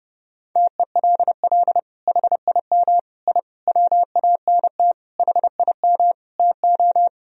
Morse code, heard as a keyed tone, about 30 wpm, 700 Hz. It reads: TELL HIM I WANT HIM TO
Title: Sentences from Top 1000 Words - Review Part 2 - 30wpm